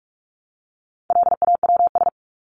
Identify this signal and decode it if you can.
Morse code, heard as a keyed tone, about 44 wpm, 700 Hz. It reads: LAWS
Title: ES:S Suffix - Top 100 Words with Base Words 45wpm